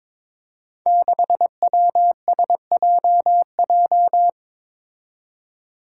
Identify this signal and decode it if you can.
Morse code, heard as a keyed tone, about 22 wpm, 700 Hz. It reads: 6WSJJ
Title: Five Challenging Letter-Number Combinations 22wpm